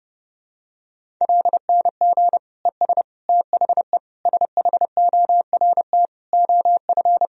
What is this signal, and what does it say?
Morse code, heard as a keyed tone, about 30 wpm, 700 Hz. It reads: LNG ES THE SHORT OF
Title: Sentences from Top 400 Words and Abbreviations 30wpm